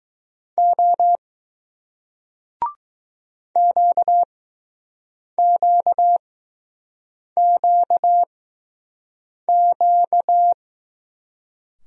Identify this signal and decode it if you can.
Morse code, heard as a keyed tone, about 20 wpm, spaced as for 6 wpm, 700 Hz. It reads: O QQQQ